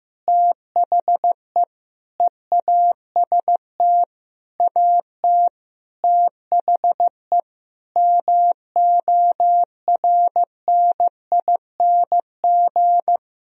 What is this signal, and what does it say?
Morse code, heard as a keyed tone, about 15 wpm, 700 Hz. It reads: THE EAST AT THE MORNING
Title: Sentences from Top 1000 Words - Review Part 3 - 15wpm